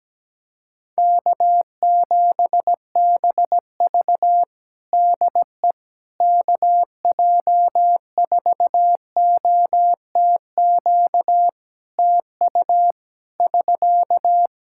Morse code, heard as a keyed tone, about 17 wpm, 700 Hz. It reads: K7BV DE KJ4OTQ TU <SK>
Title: Practice QSO - Exchange 4 - Ending - 17wpm